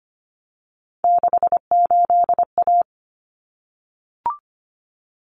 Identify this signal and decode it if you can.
Morse code, heard as a keyed tone, about 25 wpm, 700 Hz. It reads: 68A
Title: Common Cut Numbers - 3-Digit 25wpm